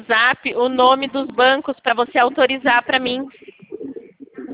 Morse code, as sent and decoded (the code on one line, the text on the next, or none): none